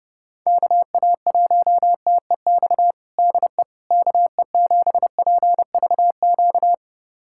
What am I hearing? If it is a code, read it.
Morse code, heard as a keyed tone, about 31 wpm, 700 Hz. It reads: KA1TEX DE KE7PVQ
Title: USA Callsign DE Callsign 30wpm